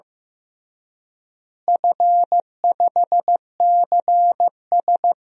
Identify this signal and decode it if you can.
Morse code, heard as a keyed tone, about 16 wpm, 700 Hz. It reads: F5CS
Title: DXCC-100 Call Signs - 4 Characters - Repeated 3x - 15wpm